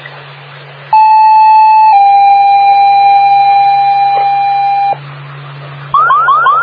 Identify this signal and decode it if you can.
Two-tone page, heard as a one-second tone, then a three-second tone, about 870 then 800 Hz. It is heard from 0.9 to 4.9 s.